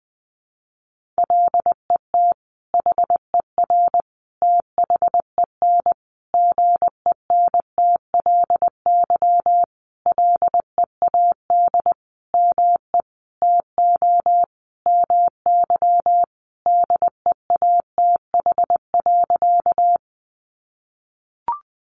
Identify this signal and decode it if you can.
Morse code, heard as a keyed tone, about 20 wpm, 700 Hz. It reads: LET HER THEN GENTLY LEAD ME TO MY DEATH.
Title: Sentences from Top 1500 Words 20wpm